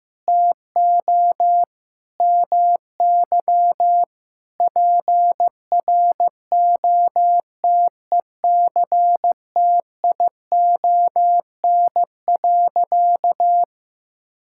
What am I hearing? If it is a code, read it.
Morse code, heard as a keyed tone, about 15 wpm, 700 Hz. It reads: TO MY PROTECTION.
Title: Sentences from Top 1600 Words 15wpm